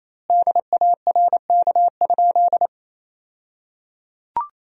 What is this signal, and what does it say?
Morse code, heard as a keyed tone, about 28 wpm, 700 Hz. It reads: DARK?
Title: 2x Word-Spacing - Sentences from Top 500 Words - 28wpm